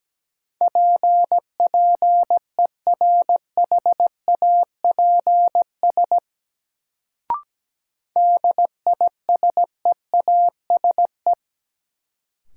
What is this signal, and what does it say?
Morse code, heard as a keyed tone, about 17 wpm, 700 Hz. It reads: PPERHAPS DISEASE
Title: Sentences from Top 1100 Words - New Word Drill Alt - 17wpm